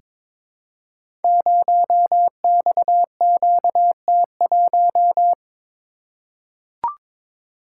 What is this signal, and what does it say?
Morse code, heard as a keyed tone, about 22 wpm, 700 Hz. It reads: 0XQT1
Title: Code Groups - 5 Letter:Numbers - 22wpm